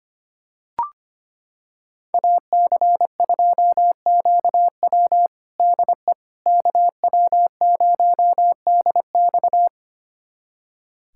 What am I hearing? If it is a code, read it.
Morse code, heard as a keyed tone, about 25 wpm, 700 Hz. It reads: AC2QW DE KW0DX
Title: USA Callsign DE Callsign - 5 Characters - 25wpm